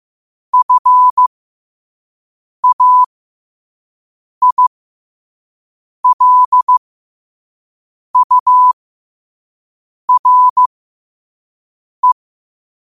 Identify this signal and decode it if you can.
Morse code, heard as a keyed tone, about 15 wpm, spaced as for 5 wpm, 1000 Hz. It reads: FAILURE
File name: qotdfiles / 5wpm